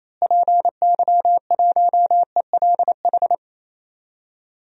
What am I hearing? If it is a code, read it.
Morse code, heard as a keyed tone, about 28 wpm, 700 Hz. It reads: PY1ELH
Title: DXCC-100 Call Signs - 6 Characters - Repeated 3x - 28wpm